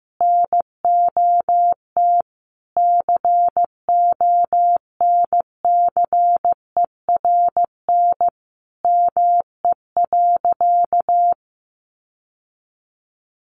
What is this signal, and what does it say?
Morse code, heard as a keyed tone, about 15 wpm, 700 Hz. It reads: NOT CONCERN ME.
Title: Sentences from Top 1000 Words 15wpm